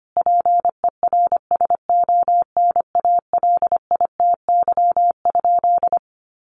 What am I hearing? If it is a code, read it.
Morse code, heard as a keyed tone, about 25 wpm, 700 Hz. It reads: PERSONALITY?
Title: Sentences from Top 1400 Words 25wpm